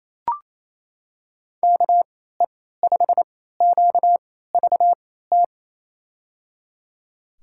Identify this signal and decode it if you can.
Morse code, heard as a keyed tone, about 28 wpm, spaced as for 16 wpm, 700 Hz. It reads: KE5QVT
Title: USA Call Signs - 6 Characters - 3x Character Spacing - 28wpm